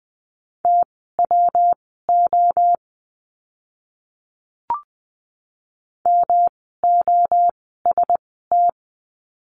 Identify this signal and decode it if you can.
Morse code, heard as a keyed tone, about 20 wpm, spaced as for 14 wpm, 700 Hz. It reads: TWO MOST
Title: 139 Lesson 23f - Letter f - Words 20wpm